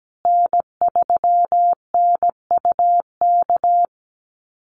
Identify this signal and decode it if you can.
Morse code, heard as a keyed tone, about 17 wpm, 700 Hz. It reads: N3NUK
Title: USA Call Signs Repeated 2x 17wpm